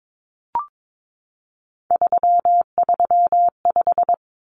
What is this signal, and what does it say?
Morse code, heard as a keyed tone, about 22 wpm, 700 Hz. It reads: 335
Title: Numbers - 3-Digit 22wpm